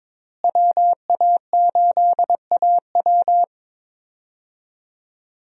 Morse code, heard as a keyed tone, about 22 wpm, 700 Hz. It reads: WA8AW
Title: USA Call Signs - 5 Characters - Repeated 2x - 22wpm